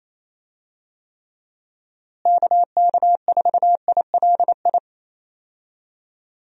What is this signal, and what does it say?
Morse code, heard as a keyed tone, about 27 wpm, 700 Hz. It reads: KK4ILI
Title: USA Call Signs - 6 Characters - Repeated 3x - 28wpm